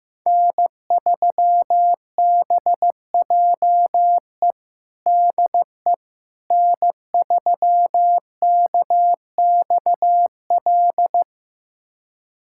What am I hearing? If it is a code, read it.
Morse code, heard as a keyed tone, about 15 wpm, 700 Hz. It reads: N3BJE DE N3KXL
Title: USA Callsign DE Callsign - 5 Characters - 15wpm